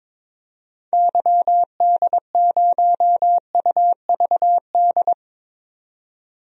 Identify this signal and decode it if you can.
Morse code, heard as a keyed tone, about 21 wpm, 700 Hz. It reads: YD0UVD
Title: DXCC-100 Call Signs - 6 Characters - 22wpm